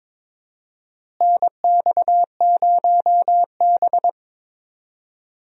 Morse code, heard as a keyed tone, about 22 wpm, 700 Hz. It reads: NX0B
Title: USA Call Signs - 4 Characters - Repeated 3x - 22wpm